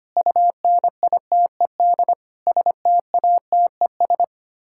Morse code, heard as a keyed tone, about 25 wpm, 700 Hz. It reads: UNITED STATES
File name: Sets of 9 Words - 25wpm